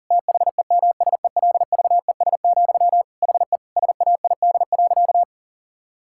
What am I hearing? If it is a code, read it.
Morse code, heard as a keyed tone, about 37 wpm, 700 Hz. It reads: THEMSELVES, HE SAID.
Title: Sentences from Top 1000 Words - Review Part 6 - 40wpm